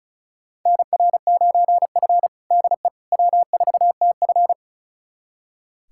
Morse code, heard as a keyed tone, about 36 wpm, 700 Hz. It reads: NR9F DE W4TF
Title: USA Callsign DE Callsign - 4 Characters - 35wpm